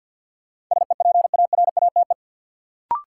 Morse code, heard as a keyed tone, about 53 wpm, 700 Hz. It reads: SEPARATE
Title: Sentences from Top 1000 Words - Review Part 7 - New Word Drill - 50wpm